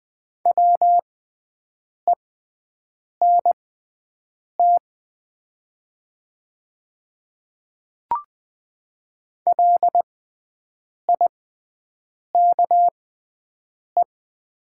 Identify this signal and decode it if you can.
Morse code, heard as a keyed tone, about 20 wpm, spaced as for 7 wpm, 700 Hz. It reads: WENT LIKE
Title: 6x Character Spacing - Top 200 Words - 20wpm